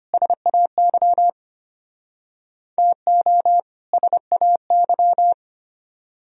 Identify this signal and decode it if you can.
Morse code, heard as a keyed tone, about 32 wpm, 700 Hz. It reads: SAY TO SAY